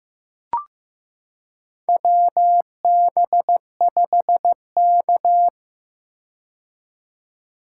Morse code, heard as a keyed tone, about 15 wpm, 700 Hz. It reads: WB5K
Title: K1USN SST QSO Practice - 15wpm